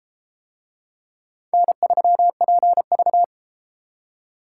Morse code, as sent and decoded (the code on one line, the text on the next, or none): -. ...-- .--. ...-
N3PV